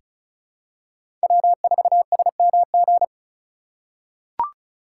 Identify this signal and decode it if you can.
Morse code, heard as a keyed tone, about 36 wpm, 700 Hz. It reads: W4SMG
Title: USA Call Signs - 5 Characters - 4x Character Spacing - 35wpm